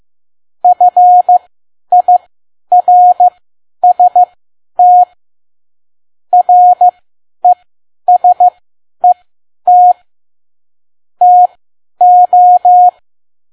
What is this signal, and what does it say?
Morse code, heard as a keyed tone, about 15 wpm, spaced as for 10 wpm, 720 Hz. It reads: FIRST RESET TO